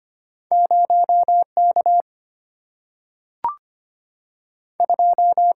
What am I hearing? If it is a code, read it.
Morse code, heard as a keyed tone, about 25 wpm, 700 Hz. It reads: 0 K 2
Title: Single-Letter Single-Number 25wpm